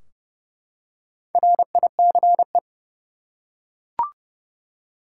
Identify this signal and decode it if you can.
Morse code, heard as a keyed tone, about 32 wpm, 700 Hz. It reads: RICE
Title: Four Letter Words - 30wpm